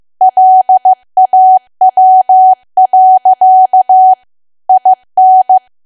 Morse code, heard as a keyed tone, about 15 wpm, 750 Hz. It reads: LAW. IN
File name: cwa / W1AW-Level2 / 15WPM